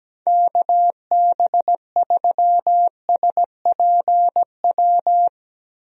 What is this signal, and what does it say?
Morse code, heard as a keyed tone, about 17 wpm, 700 Hz. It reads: KB3SPW
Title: USA Call Signs - 6 Characters - Repeated 3x - 17wpm